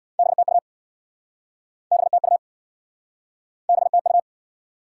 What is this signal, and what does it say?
Morse code, heard as a keyed tone, about 71 wpm, 700 Hz. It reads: BTU BTU BTU